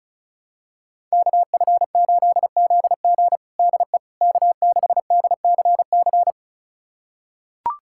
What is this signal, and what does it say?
Morse code, heard as a keyed tone, about 36 wpm, 700 Hz. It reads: KF8ZG DE K6DCC
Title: USA Callsign DE Callsign - 5 Characters - 35wpm